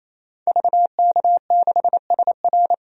Morse code, heard as a keyed tone, about 28 wpm, 700 Hz. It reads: VK6SR
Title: DXCC-100 Call Signs - 5 Characters - 28wpm